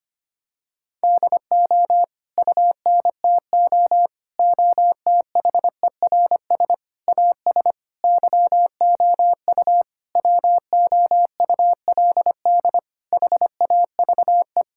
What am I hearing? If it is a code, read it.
Morse code, heard as a keyed tone, about 25 wpm, 700 Hz. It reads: DO UNTO OTHERS AS YOU WOULD HAVE
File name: practice / 25wpm / Sentences with 9-Word Phrases - 25wpm